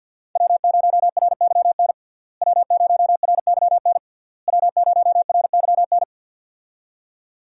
Morse code, heard as a keyed tone, about 49 wpm, 700 Hz. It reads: W0RYN W0RYN W0RYN
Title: USA Call Signs - 5 Characters - Repeated 3x - 50wpm